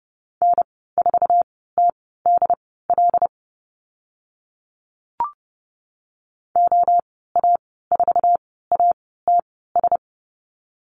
Morse code, heard as a keyed tone, about 30 wpm, spaced as for 17 wpm, 700 Hz. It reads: N4TDL OA4ATS
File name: Lesson 15g - Number 5 - Call Signs 30wpm